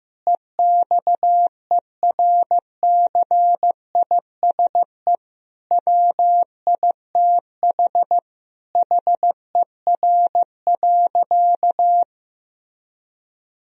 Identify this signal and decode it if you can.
Morse code, heard as a keyed tone, about 15 wpm, 700 Hz. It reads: EXERCISE WITH HER.
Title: Sentences from Top 1000 Words - Review Part 4 - 15wpm